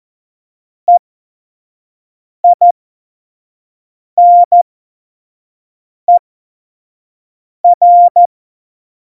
Morse code, heard as a keyed tone, about 14 wpm, spaced as for 5 wpm, 700 Hz. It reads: EINER
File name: Morsekurs / Witze / Witz 24 April 2018, 25 F70